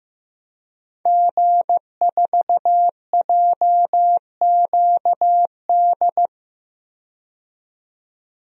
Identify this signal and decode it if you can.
Morse code, heard as a keyed tone, about 15 wpm, 700 Hz. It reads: G4JQD
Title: DXCC-100 Call Signs - 5 Characters - 15wpm